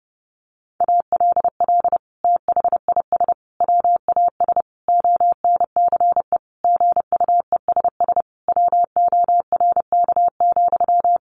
Morse code, heard as a keyed tone, about 30 wpm, 700 Hz. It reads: ALL THIS WAS ONCE GUESS WORK,
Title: Sentences from Top 1500 Words 30wpm